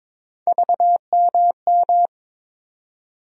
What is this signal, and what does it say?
Morse code, heard as a keyed tone, about 22 wpm, 700 Hz. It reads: VMM
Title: USA Call Sign Postfixes - 3 Letters 22wpm